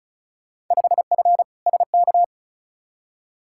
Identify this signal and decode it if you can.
Morse code, heard as a keyed tone, about 36 wpm, 700 Hz. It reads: 5F SK
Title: Field Day Exchanges Encore - 35wpm